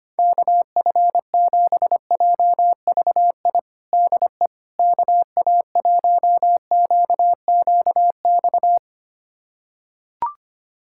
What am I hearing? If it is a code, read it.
Morse code, heard as a keyed tone, about 25 wpm, 700 Hz. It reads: KF7JVI DE KA1QQX
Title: USA Callsign DE Callsign - 6 Characters - 25wpm